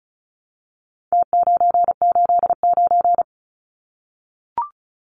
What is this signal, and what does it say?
Morse code, heard as a keyed tone, about 34 wpm, 700 Hz. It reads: T989
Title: Numbers - Contest Style - 4-Digit 35wpm